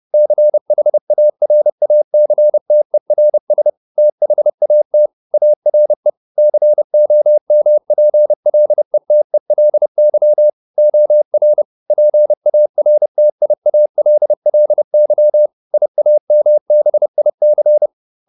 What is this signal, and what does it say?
Morse code, heard as a keyed tone, about 30 wpm, 600 Hz. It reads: CHARACTERS THAT ARE COMPLETELY OR PARTIALLY IAMBIC